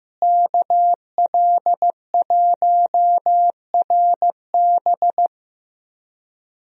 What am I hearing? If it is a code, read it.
Morse code, heard as a keyed tone, about 15 wpm, 700 Hz. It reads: KL1RB
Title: USA Call Signs - 5 Characters - Repeated 3x - 15wpm